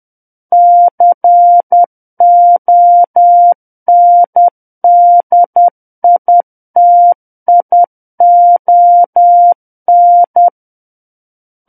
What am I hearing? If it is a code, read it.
Morse code, heard as a keyed tone, about 10 wpm, 700 Hz. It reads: CONDITION
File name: Book One / 01-Chapter 1 - 10 WPM